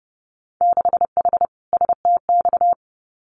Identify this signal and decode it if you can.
Morse code, heard as a keyed tone, about 30 wpm, 700 Hz. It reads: 6H STX
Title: Winter Field Day Exchanges Encore 30wpm